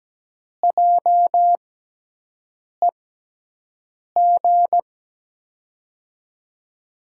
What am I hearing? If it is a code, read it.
Morse code, heard as a keyed tone, about 17 wpm, spaced as for 6 wpm, 700 Hz. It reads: JEG
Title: USA Call Signs - 6 Characters - 6x Character Spacing - 17wpm